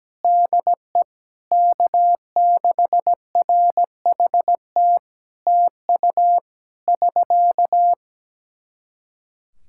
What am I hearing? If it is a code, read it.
Morse code, heard as a keyed tone, about 17 wpm, 700 Hz. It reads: DE K6RHT TU <SK>
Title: Practice QSO - Exchange 4 - Ending - 17wpm